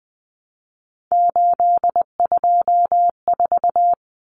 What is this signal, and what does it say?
Morse code, heard as a keyed tone, about 20 wpm, 700 Hz. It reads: 824
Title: Common Cut Numbers - 3-Digit 20wpm